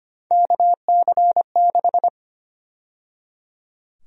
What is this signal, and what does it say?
Morse code, heard as a keyed tone, about 25 wpm, 700 Hz. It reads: KC6
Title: USA Call Sign Prefixes - 25wpm